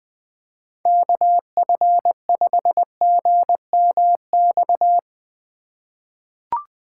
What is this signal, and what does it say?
Morse code, heard as a keyed tone, about 20 wpm, 700 Hz. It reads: KF5GMX
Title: USA Call Signs - 6 Characters - 2x Character Spacing - 20wpm